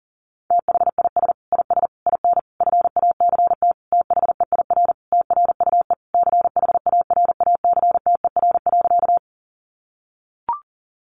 Morse code, heard as a keyed tone, about 40 wpm, 700 Hz. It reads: THIS IS IN FACT THEIR TRUE CHARACTER.
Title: Sentences from Top 1200 Words 40wpm